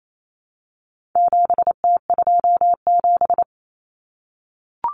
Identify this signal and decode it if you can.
Morse code, heard as a keyed tone, about 28 wpm, 700 Hz. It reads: 7T27
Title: Common Cut Numbers - 4-Digit 28wpm